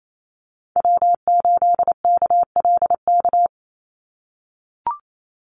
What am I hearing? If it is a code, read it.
Morse code, heard as a keyed tone, about 28 wpm, 700 Hz. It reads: W8KLK
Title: USA Call Signs 28wpm